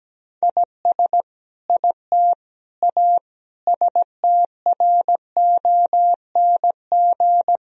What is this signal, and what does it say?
Morse code, heard as a keyed tone, about 17 wpm, 700 Hz. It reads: IS IT A STRONG
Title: Sentences from Top 1000 Words - Review Part 4 - 17wpm